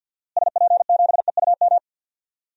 Morse code, heard as a keyed tone, about 48 wpm, 700 Hz. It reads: SP7EUM
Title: DXCC-100 Call Signs - 6 Characters - Repeated 2x - 50wpm